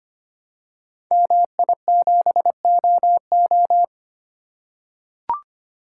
Morse code, heard as a keyed tone, about 25 wpm, 700 Hz. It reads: MI7OO
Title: Code Groups - 5 Letter:Numbers - 25wpm